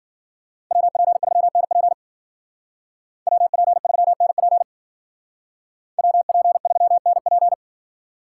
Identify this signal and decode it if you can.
Morse code, heard as a keyed tone, about 49 wpm, 700 Hz. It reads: WP3NP WP3NP WP3NP
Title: Speed Racing - USA 2x2 Call Signs 40wpm